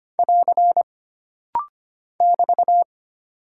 Morse code, heard as a keyed tone, about 25 wpm, 700 Hz. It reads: <AR> <BT>
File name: Single Punctuation-Prosign - Rapid-Fire - 25wpm